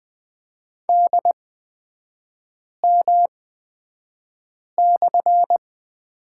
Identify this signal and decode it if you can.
Morse code, heard as a keyed tone, about 20 wpm, spaced as for 5 wpm, 700 Hz. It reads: DM/
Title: 229 Lesson 37e - Letter x - Three Characters 20wpm